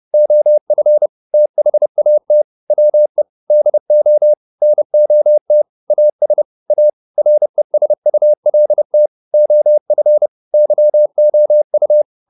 Morse code, heard as a keyed tone, about 30 wpm, 600 Hz. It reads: OF THAT WE DO NOT AS A RESULT OF YOU